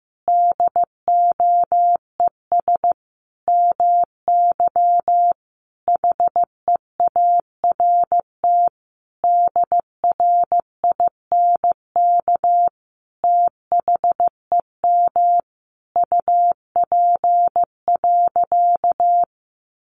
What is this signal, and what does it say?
Morse code, heard as a keyed tone, about 15 wpm, 700 Hz. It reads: DOES MY HEART DRINK THEM UP.
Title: Sentences from Top 1500 Words 15wpm